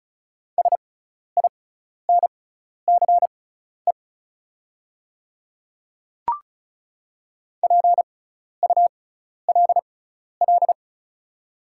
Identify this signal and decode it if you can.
Morse code, heard as a keyed tone, about 36 wpm, spaced as for 12 wpm, 700 Hz. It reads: SINCE PULL
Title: 6x Character Spacing - Top 500 Words - 35wpm